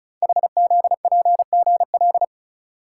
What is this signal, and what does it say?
Morse code, heard as a keyed tone, about 36 wpm, 700 Hz. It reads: HZPGL
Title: Code Groups - 5 Letters - 35wpm